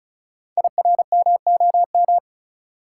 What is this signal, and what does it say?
Morse code, heard as a keyed tone, about 36 wpm, 690 Hz. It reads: IRMOM
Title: Code Groups - 5 Letters - 35wpm